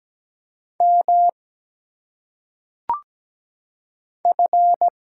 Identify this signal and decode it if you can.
Morse code, heard as a keyed tone, about 17 wpm, 700 Hz. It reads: M F
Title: Single Letters - Phonetic Pronunciation 17wpm